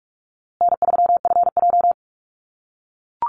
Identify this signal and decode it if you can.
Morse code, heard as a keyed tone, about 44 wpm, 700 Hz. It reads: N3FJ
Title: USA Call Signs Repeated 3x 45wpm